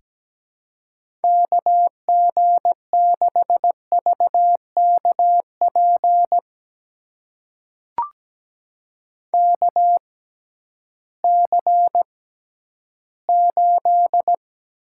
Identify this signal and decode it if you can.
Morse code, heard as a keyed tone, about 17 wpm, spaced as for 11 wpm, 700 Hz. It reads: KG6VKP KC8
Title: USA Call Signs - 6 Characters - 6x Character Spacing - 17wpm